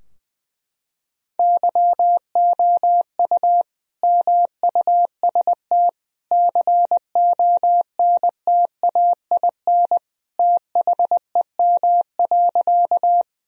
Morse code, heard as a keyed tone, about 20 wpm, 700 Hz. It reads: YOU MUST CONTAIN THEM.
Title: Sentences from Top 1000 Words - Review Part 2 - 20wpm